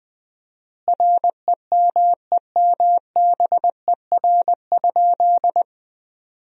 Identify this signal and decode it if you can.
Morse code, heard as a keyed tone, about 20 wpm, 700 Hz. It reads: REMEMBER?
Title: 4x Word-Spacing - Sentences from Top 400 Words - 20wpm